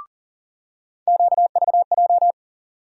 Morse code, heard as a keyed tone, about 40 wpm, 700 Hz. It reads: QVJ
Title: USA Call Sign Postfixes - 3 Letters 40wpm